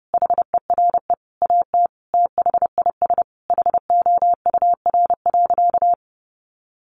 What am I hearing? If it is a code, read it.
Morse code, heard as a keyed tone, about 30 wpm, 700 Hz. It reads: HERE AT THIS HOUR.
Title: Sentences from Top 1400 Words 30wpm